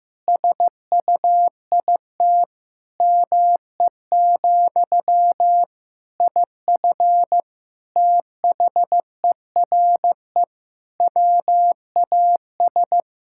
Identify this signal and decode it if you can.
Morse code, heard as a keyed tone, about 15 wpm, 700 Hz. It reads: SUIT ME, IF THERE WAS
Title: Sentences from Top 1600 Words 15wpm